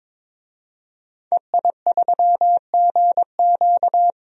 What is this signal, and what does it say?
Morse code, heard as a keyed tone, about 21 wpm, 700 Hz. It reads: EI3GQ
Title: DXCC-100 Call Signs - 5 Characters - 22wpm